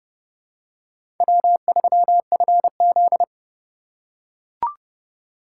Morse code, heard as a keyed tone, about 32 wpm, 700 Hz. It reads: W3FZ
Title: USA Call Signs - 4 Characters - 4x Character Spacing - 30wpm